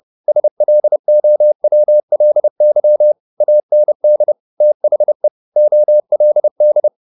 Morse code, heard as a keyed tone, about 30 wpm, 600 Hz. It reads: SLOWLY AND THE OLD
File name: the old man and the sea / the old man and the sea 1 30wpm